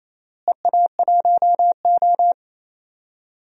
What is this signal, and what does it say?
Morse code, heard as a keyed tone, about 28 wpm, 700 Hz. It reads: EA1O
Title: DXCC-100 Call Signs - 4 Characters - Repeated 2x - 28wpm